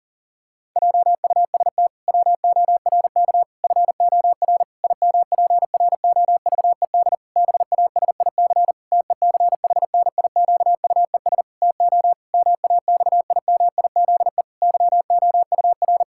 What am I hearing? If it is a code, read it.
Morse code, heard as a keyed tone, about 40 wpm, 700 Hz. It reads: JUST WORK FOR IMPROVED BASIC TECHNIQUES TO MAXIMIZE YOUR
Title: Pangram Challenge v2 - 40wpm